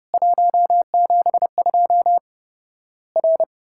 Morse code, heard as a keyed tone, about 30 wpm, 700 Hz. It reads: 172 R
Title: Worked All Europe DX Contest - QTC Exchange - 30wpm